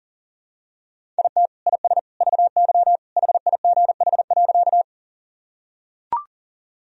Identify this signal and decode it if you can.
Morse code, heard as a keyed tone, about 40 wpm, 700 Hz. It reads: IT IS VY HIGH.
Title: Sentences from Top 200 Words and Abbreviations 40wpm